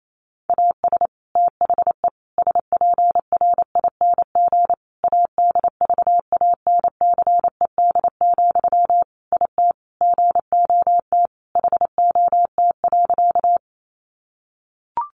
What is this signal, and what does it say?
Morse code, heard as a keyed tone, about 28 wpm, 700 Hz. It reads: AS THE SPRING ADVANCED, IT GOT HOT.
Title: Sentences from Top 1200 Words 28wpm